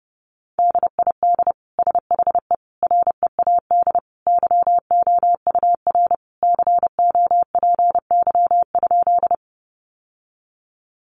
Morse code, heard as a keyed tone, about 30 wpm, 700 Hz. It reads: DID SHE READ YOUR COPY?
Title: Sentences from Top 1400 Words 30wpm